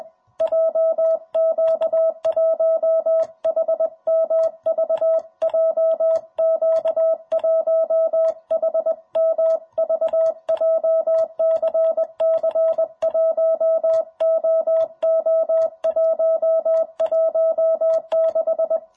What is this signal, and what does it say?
Morse code, heard as a keyed tone, about 21 wpm, 660 Hz. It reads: JQ1HMV JQ1HMV JCC 1OO116
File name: jh8cbh / 2021cw